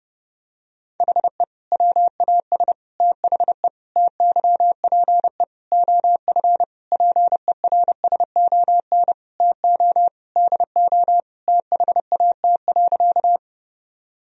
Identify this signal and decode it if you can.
Morse code, heard as a keyed tone, about 30 wpm, 700 Hz. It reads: HE WAS THE TYPE OF PERSON TO DO THAT.
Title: Sentences from Top 1000 Words - Review Part 3 - 30wpm